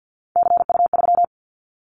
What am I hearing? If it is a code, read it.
Morse code, heard as a keyed tone, about 50 wpm, 700 Hz. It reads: CU3
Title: Callsign Prefixes for Top 50 DXCC - 50wpm